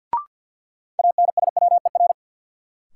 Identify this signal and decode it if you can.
Morse code, heard as a keyed tone, about 49 wpm, 700 Hz. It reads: ANSWER
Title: Sentences from Top 300 Words - New Word Drill - 50wpm